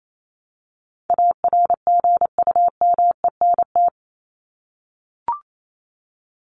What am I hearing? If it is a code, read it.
Morse code, heard as a keyed tone, about 28 wpm, 700 Hz. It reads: ARGUMENT
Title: Top 1251-1500 Words 28wpm